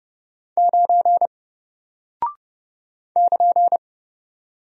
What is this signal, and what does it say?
Morse code, heard as a keyed tone, about 30 wpm, 700 Hz. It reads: 9 <KN>
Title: Single Character (Letter:Number:Punctuation:Prosign) Alt 30wpm